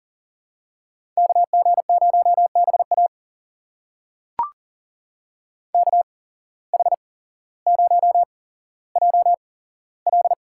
Morse code, heard as a keyed tone, about 40 wpm, spaced as for 17 wpm, 700 Hz. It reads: KG0BA KH0JL
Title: USA Call Signs - 5 Characters - 8x Character Spacing - 40wpm